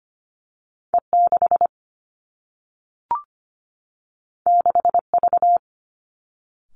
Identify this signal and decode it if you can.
Morse code, heard as a keyed tone, about 25 wpm, 700 Hz. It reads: E6 6V
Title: Two Challenging Letter-Number Combinations 25wpm